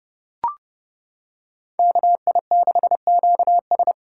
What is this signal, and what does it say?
Morse code, heard as a keyed tone, about 30 wpm, 700 Hz. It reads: KI6QS